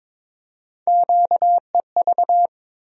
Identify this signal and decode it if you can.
Morse code, heard as a keyed tone, about 22 wpm, 700 Hz. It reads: QEV
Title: USA Call Sign Postfixes - 3 Letters 22wpm